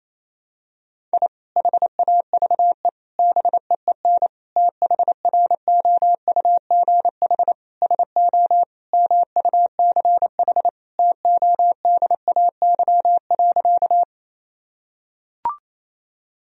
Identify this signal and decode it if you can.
Morse code, heard as a keyed tone, about 28 wpm, 700 Hz. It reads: I HAVE BEEN THROUGH SO MUCH TODAY.
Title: Sentences from Top 1000 Words - Review Part 4 - 28wpm